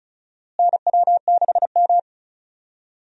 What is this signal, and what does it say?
Morse code, heard as a keyed tone, about 36 wpm, 700 Hz. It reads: NW6M
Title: USA Call Signs - 4 Characters - Repeated 3x - 35wpm